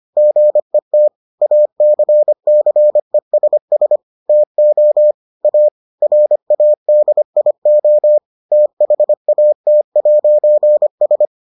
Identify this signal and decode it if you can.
Morse code, heard as a keyed tone, about 25 wpm, 600 Hz. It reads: GET ACCESS TO A RADIO THAT'S